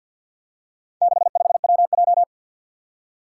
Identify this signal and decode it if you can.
Morse code, heard as a keyed tone, about 47 wpm, 700 Hz. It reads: 65WJ
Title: Four Challenging Letter-Number Combinations 50wpm